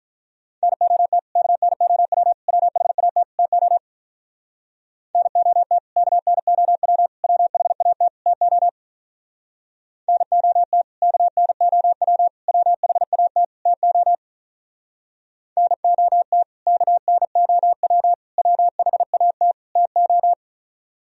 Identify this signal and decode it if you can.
Morse code, heard as a keyed tone, about 44 wpm, 700 Hz. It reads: NOT KNOW WHAT TO NOT KNOW WHAT TO NOT KNOW WHAT TO NOT KNOW WHAT TO